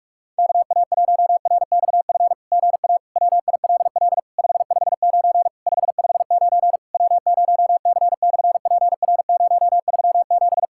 Morse code, heard as a keyed tone, about 46 wpm, 690 Hz. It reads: KA1RXF GA WILL 559 559 W0C/PR037
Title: SOTA QSO Practice 45wpm